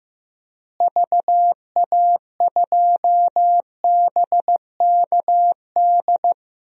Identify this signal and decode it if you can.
Morse code, heard as a keyed tone, about 15 wpm, 700 Hz. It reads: VA2BKD
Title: DXCC-100 Call Signs - 6 Characters - Repeated 2x - 15wpm